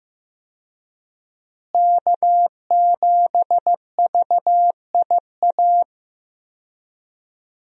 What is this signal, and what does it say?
Morse code, heard as a keyed tone, about 15 wpm, 700 Hz. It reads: K7VIA
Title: USA Call Signs - 5 Characters - Repeated 2x - 15wpm